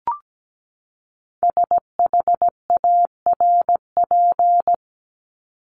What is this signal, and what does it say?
Morse code, heard as a keyed tone, about 17 wpm, 700 Hz. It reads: SHARP
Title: Top 1501-1750 Words 17wpm